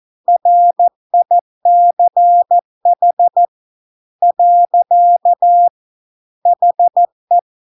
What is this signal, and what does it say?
Morse code, heard as a keyed tone, about 14 wpm, 700 Hz. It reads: RICH . HE